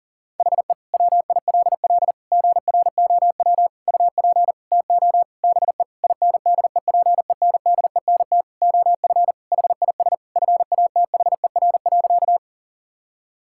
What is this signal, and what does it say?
Morse code, heard as a keyed tone, about 40 wpm, 700 Hz. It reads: HE WILL GROW UP TO BE INDEPENDENT OF HIS FATHER.
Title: Sentences from Top 1000 Words - Review Part 6 - 40wpm